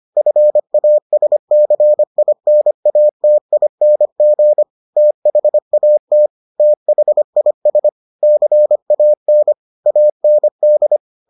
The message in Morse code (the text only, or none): FASCINATING THAT THIS CAN AND